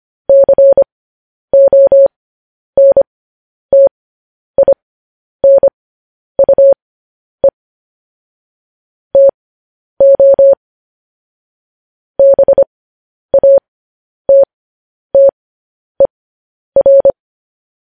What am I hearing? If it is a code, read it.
Morse code, heard as a keyed tone, about 25 wpm, spaced as for 10 wpm, 550 Hz. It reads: CONTINUE TO BATTER
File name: podcast / News headlines-1x-25x10